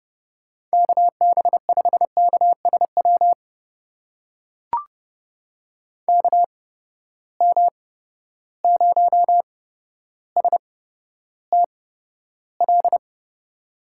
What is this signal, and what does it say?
Morse code, heard as a keyed tone, about 30 wpm, spaced as for 13 wpm, 700 Hz. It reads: KB5KSW KM0STL
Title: USA Call Signs - 6 Characters - 8x Character Spacing - 30wpm